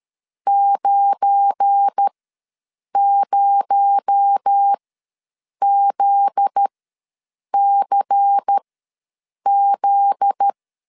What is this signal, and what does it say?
Morse code, heard as a keyed tone, about 13 wpm, spaced as for 7 wpm, 800 Hz. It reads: 90ZCZ
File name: LEZIONI / 7 GF9Z / 7 GF9Z ws 12wpm 3321